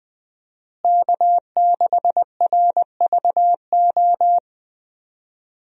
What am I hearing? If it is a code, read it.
Morse code, heard as a keyed tone, about 21 wpm, 700 Hz. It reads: K6RVO
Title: USA Call Signs - 5 Characters - Repeated 3x - 20wpm